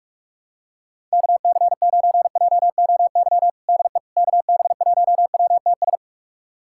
Morse code, heard as a keyed tone, about 41 wpm, 700 Hz. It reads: KC9JOY DE KB1WTS
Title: USA Callsign DE Callsign - 6 Characters - 45wpm